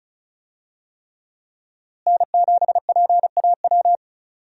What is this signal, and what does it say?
Morse code, heard as a keyed tone, about 36 wpm, 700 Hz. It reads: N7PAW
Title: USA Call Signs - 5 Characters - Repeated 3x - 35wpm